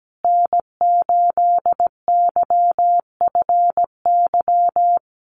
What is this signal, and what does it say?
Morse code, heard as a keyed tone, about 17 wpm, 700 Hz. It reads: N8YFY
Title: USA Call Signs Repeated 2x 17wpm